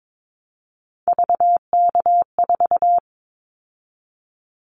Morse code, heard as a keyed tone, about 22 wpm, 700 Hz. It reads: VK4
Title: Callsign Prefixes for Top 100 DXCC - 22wpm